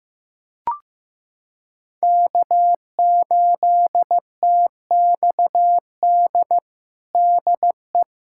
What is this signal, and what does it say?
Morse code, heard as a keyed tone, about 14 wpm, 700 Hz. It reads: K8TXD DE
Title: USA Callsign DE Callsign - 5 Characters - 15wpm